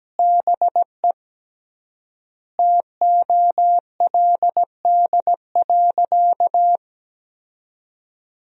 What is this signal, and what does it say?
Morse code, heard as a keyed tone, about 17 wpm, 700 Hz. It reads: BE TOLD.
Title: 3x Word-Spacing - Sentences from Top 400 Words - 17wpm